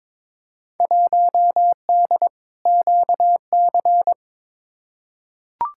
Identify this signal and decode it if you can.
Morse code, heard as a keyed tone, about 22 wpm, 700 Hz. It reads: 1D QC
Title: Field Day Exchanges - 22wpm